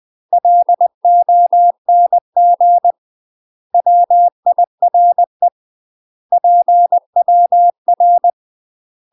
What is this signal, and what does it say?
Morse code, heard as a keyed tone, about 20 wpm, 700 Hz. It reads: LONG WIRE PWR
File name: UK MASTER QSOs TRAINING / 20WPM / UK MASTER QSOs 0006